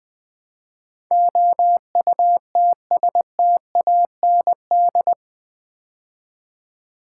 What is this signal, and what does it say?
Morse code, heard as a keyed tone, about 20 wpm, 700 Hz. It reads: OUTSTAND
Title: Build A Word - 4 Letters At A Time - 11-Letter Words - 20wpm